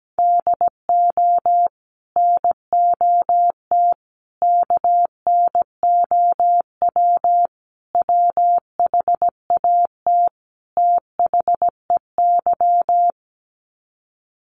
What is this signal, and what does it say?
Morse code, heard as a keyed tone, about 17 wpm, 700 Hz. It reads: DO NOT KNOW WHAT THEY
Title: Sets of 5 Words 17wpm